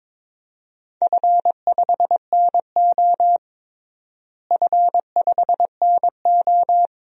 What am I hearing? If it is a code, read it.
Morse code, heard as a keyed tone, about 22 wpm, 700 Hz. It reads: F5NO F5NO
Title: DXCC-100 Call Signs - 4 Characters - Repeated 2x - 22wpm